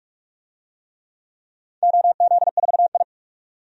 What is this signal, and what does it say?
Morse code, heard as a keyed tone, about 46 wpm, 700 Hz. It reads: OZ4I
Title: DXCC-100 Call Signs - 4 Characters - 45wpm